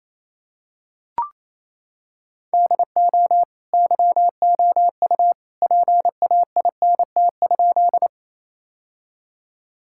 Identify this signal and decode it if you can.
Morse code, heard as a keyed tone, about 28 wpm, 700 Hz. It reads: DO YOU PAINT?
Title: Sentences from Top 1000 Words - Review Part 6 - 28wpm